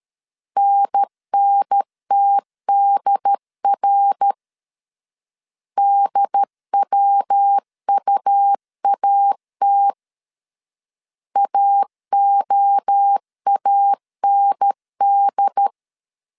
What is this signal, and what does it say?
Morse code, heard as a keyed tone, about 13 wpm, 800 Hz. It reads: NNTDR DWUAT AOAND
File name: CW / DET / 1 3 1500C S B 12wpm 2828